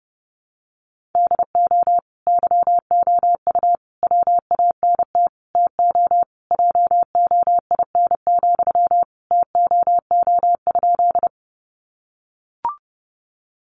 Morse code, heard as a keyed tone, about 30 wpm, 700 Hz. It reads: DO YOU WANT TO JOIN, TOO?
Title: Sentences from Top 1400 Words 30wpm